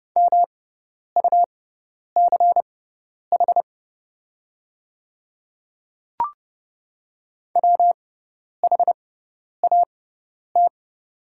Morse code, heard as a keyed tone, about 30 wpm, spaced as for 10 wpm, 700 Hz. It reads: MUCH WHAT
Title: 6x Character Spacing - Top 300 Words - 30wpm